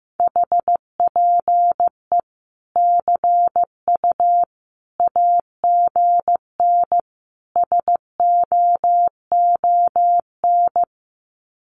Morse code, heard as a keyed tone, about 15 wpm, 700 Hz. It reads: HPE CU AGN SOON
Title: Common QSO Phrases 15wpm